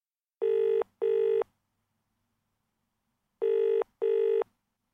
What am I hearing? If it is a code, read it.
There is no code.